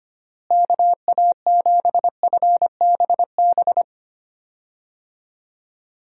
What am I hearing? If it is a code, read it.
Morse code, heard as a keyed tone, about 25 wpm, 700 Hz. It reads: KA7FBB
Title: USA Call Signs - 6 Characters - Repeated 2x - 25wpm